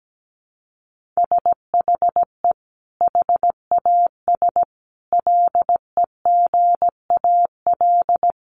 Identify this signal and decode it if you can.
Morse code, heard as a keyed tone, about 17 wpm, 700 Hz. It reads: SHE HAS LEGAL
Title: Sentences from Top 1200 Words 17wpm